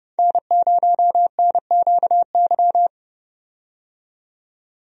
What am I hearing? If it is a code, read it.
Morse code, heard as a keyed tone, about 29 wpm, 700 Hz. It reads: N0NQY
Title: USA Call Signs - 5 Characters - 30wpm